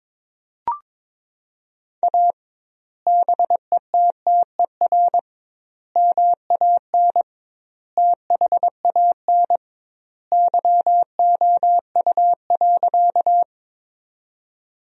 Morse code, heard as a keyed tone, about 22 wpm, 700 Hz. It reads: A BETTER MAN THAN YOU.
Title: 2x Word-Spacing - Sentences from Top 400 Words - 22wpm